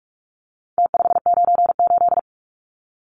Morse code, heard as a keyed tone, about 44 wpm, 700 Hz. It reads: T598
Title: Numbers - Contest Style - 4-Digit 45wpm